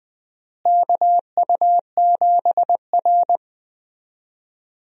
Morse code, heard as a keyed tone, about 21 wpm, 700 Hz. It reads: KU7R
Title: USA Call Signs - 4 Characters - Repeated 2x - 20wpm